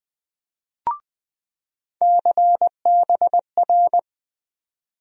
Morse code, heard as a keyed tone, about 20 wpm, 700 Hz. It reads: CBR
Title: USA Call Sign Postfixes - 3 Letters 20wpm